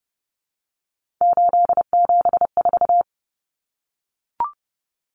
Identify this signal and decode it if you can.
Morse code, heard as a keyed tone, about 30 wpm, 700 Hz. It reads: 874
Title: Numbers - 3-Digit 30wpm